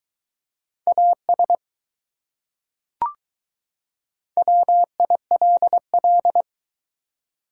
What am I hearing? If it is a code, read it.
Morse code, heard as a keyed tone, about 21 wpm, 700 Hz. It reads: AS WILL